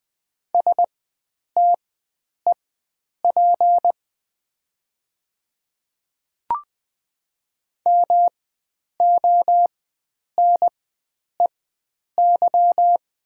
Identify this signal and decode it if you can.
Morse code, heard as a keyed tone, about 20 wpm, spaced as for 9 wpm, 700 Hz. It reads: STEP MONEY
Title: 4x Character Spacing - Top 500 Words - 20wpm